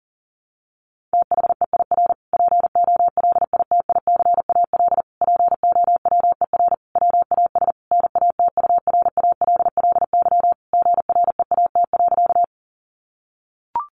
Morse code, heard as a keyed tone, about 40 wpm, 700 Hz. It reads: THEIR POLITICAL POWER WAS NATURALLY GREAT.
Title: Sentences from Top 1300 Words 40wpm